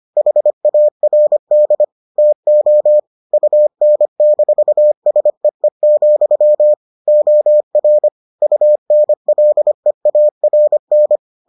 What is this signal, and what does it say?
Morse code, heard as a keyed tone, about 25 wpm, 600 Hz. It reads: HARD TO UN-SEE, OR UNLEARN